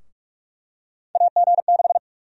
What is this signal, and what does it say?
Morse code, heard as a keyed tone, about 46 wpm, 700 Hz. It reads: AG6
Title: USA Call Sign Prefixes - 45wpm